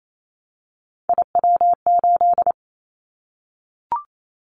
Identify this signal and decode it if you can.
Morse code, heard as a keyed tone, about 28 wpm, 700 Hz. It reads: IW8
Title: Callsign Prefixes for Top 50 DXCC - 28wpm